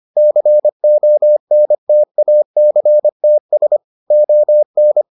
Morse code, heard as a keyed tone, about 25 wpm, 600 Hz. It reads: CONTACTS ON